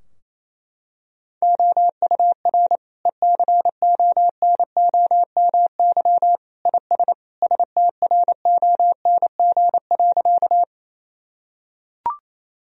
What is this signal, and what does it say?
Morse code, heard as a keyed tone, about 28 wpm, 700 Hz. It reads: OUR ECONOMY IS STRONG.
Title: Sentences from Top 1000 Words - Review Part 3 - 28wpm